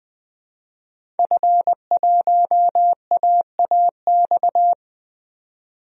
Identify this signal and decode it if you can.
Morse code, heard as a keyed tone, about 20 wpm, 700 Hz. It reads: F1AAX
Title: DXCC-100 Call Signs - 5 Characters - 20wpm